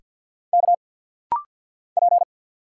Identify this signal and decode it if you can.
Morse code, heard as a keyed tone, about 48 wpm, 700 Hz. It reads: K P
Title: Single Letters - Rapid-Fire 50wpm